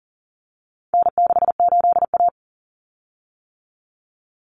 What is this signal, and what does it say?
Morse code, heard as a keyed tone, about 40 wpm, 700 Hz. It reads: N68A
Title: Common Cut Numbers - 4-Digit 40wpm